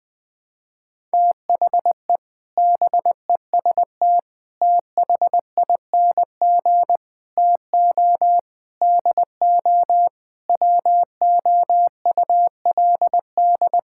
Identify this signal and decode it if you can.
Morse code, heard as a keyed tone, about 20 wpm, 700 Hz. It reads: THE BEST THING TO DO WOULD
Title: Sentences with 8-Word Phrases - Spoken Beginning - 20wpm